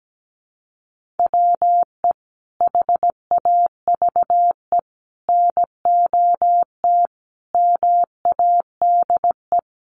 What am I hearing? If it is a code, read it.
Morse code, heard as a keyed tone, about 17 wpm, 700 Hz. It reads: WE HAVE NOT MADE
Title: Sentences from Top 1400 Words 17wpm